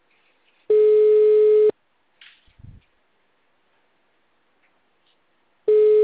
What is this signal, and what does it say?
Ringback tone (Europe, 425 Hz).